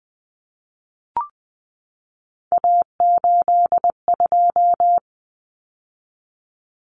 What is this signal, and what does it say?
Morse code, heard as a keyed tone, about 20 wpm, 700 Hz. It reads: A82
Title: Common Cut Numbers - 3-Digit 20wpm